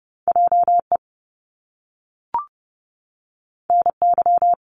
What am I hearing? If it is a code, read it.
Morse code, heard as a keyed tone, about 30 wpm, 700 Hz. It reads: JE NY
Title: Top Two Letter Combinations in Words 30wpm